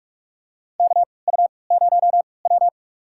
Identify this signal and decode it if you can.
Morse code, heard as a keyed tone, about 46 wpm, spaced as for 25 wpm, 700 Hz. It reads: KU0W
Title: USA Call Signs - 4 Characters - 3x Character Spacing - 45wpm